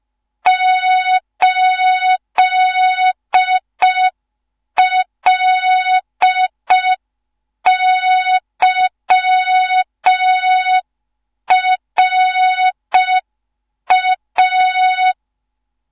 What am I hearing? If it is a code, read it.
Morse code, heard as a keyed tone, about 5 wpm, 3700 Hz. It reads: 8LYRA
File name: Morze-HA2MN / hang / 15 GYAKORLAT